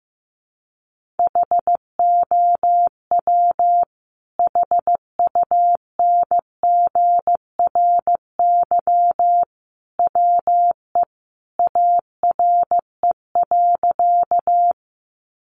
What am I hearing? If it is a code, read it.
Morse code, heard as a keyed tone, about 15 wpm, 700 Hz. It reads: HOW HUNGRY WE ARE.
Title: Sentences from Top 1400 Words 15wpm